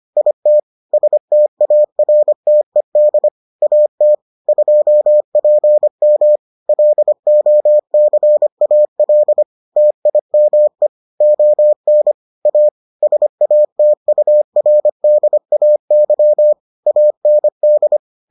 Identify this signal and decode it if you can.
Morse code, heard as a keyed tone, about 25 wpm, 600 Hz. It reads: IT STARTED AT 2PM LOCAL TIME ON A SATURDAY AND